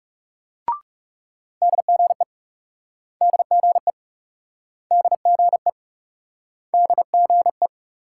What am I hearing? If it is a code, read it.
Morse code, heard as a keyed tone, about 39 wpm, 700 Hz. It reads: DGE DGE DGE DGE